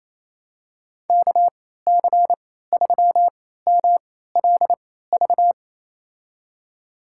Morse code, heard as a keyed tone, about 27 wpm, spaced as for 16 wpm, 700 Hz. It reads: KC3MLV
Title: USA Call Signs - 6 Characters - 3x Character Spacing - 28wpm